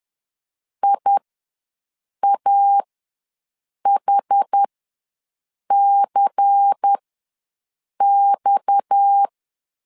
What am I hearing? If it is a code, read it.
Morse code, heard as a keyed tone, about 10 wpm, spaced as for 6 wpm, 800 Hz. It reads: IAHCX